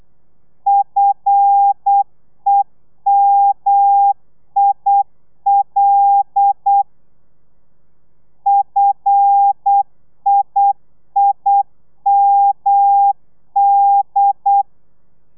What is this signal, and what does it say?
Morse code, heard as a keyed tone, about 8 wpm, 800 Hz. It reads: FEMIL FIIMD
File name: telegrafia / i2tfj / Lez5 35